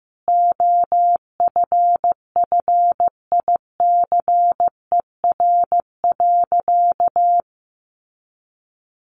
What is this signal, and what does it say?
Morse code, heard as a keyed tone, about 15 wpm, 700 Hz. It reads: OFFICER.
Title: Sentences from Top 1200 Words 15wpm